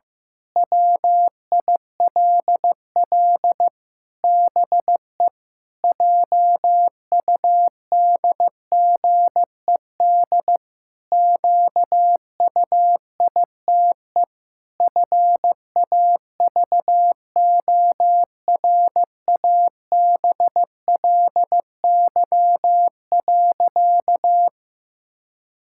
Morse code, heard as a keyed tone, about 16 wpm, 700 Hz. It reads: WILL BE JUDGED QUITE FAVORABLY.
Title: Pangram Challenge v2 - 15wpm